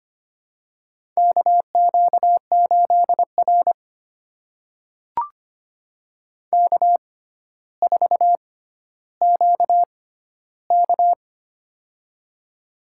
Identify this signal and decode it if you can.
Morse code, heard as a keyed tone, about 25 wpm, spaced as for 13 wpm, 700 Hz. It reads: KQ8R K4QK
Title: USA Call Signs - 6x Character Spacing - 25wpm